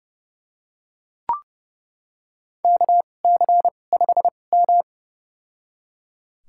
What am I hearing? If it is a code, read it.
Morse code, heard as a keyed tone, about 32 wpm, spaced as for 22 wpm, 700 Hz. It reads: KC5M
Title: USA Call Signs - 4 Characters - 2x Character Spacing - 30wpm